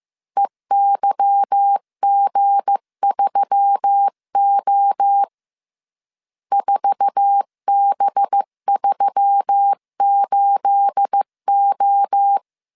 Morse code, heard as a keyed tone, about 15 wpm, 800 Hz. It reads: EYG3O 4B38O
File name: CW / DET / 1 9 1500C S B 14wpm 3302